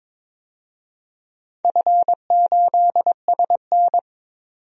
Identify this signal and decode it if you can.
Morse code, heard as a keyed tone, about 23 wpm, 700 Hz. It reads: F8SN
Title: DXCC-100 Call Signs - 4 Characters - 22wpm